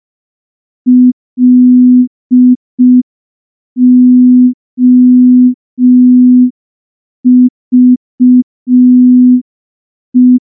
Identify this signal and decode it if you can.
Morse code, heard as a keyed tone, about 5 wpm, 250 Hz. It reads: LOVE
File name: morse / morse-training-05WPM